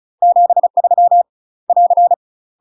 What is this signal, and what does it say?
Morse code, heard as a keyed tone, about 36 wpm, 700 Hz. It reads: 73 <AR>
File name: UK MASTER QSOs TRAINING / 35WPM / UK MASTER QSOs 0041